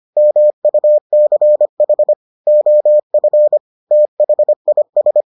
Morse code, heard as a keyed tone, about 25 wpm, 600 Hz. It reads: MUCH OF THIS